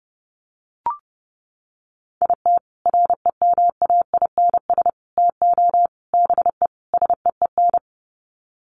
Morse code, heard as a keyed tone, about 30 wpm, 700 Hz. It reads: IT REMAINS TO BE SEEN
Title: Sets of 5 Words Encore 30wpm